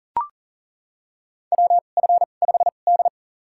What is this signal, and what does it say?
Morse code, heard as a keyed tone, about 40 wpm, spaced as for 29 wpm, 700 Hz. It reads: WF5D
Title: USA Call Signs - 4 Characters - 2x Character Spacing - 40wpm